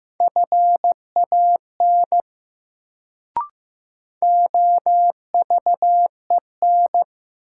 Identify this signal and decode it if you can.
Morse code, heard as a keyed tone, about 15 wpm, 700 Hz. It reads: FAN OVEN
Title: Sentences from Top 1000 Words - Review Part 4 - New Word Drill - 15wpm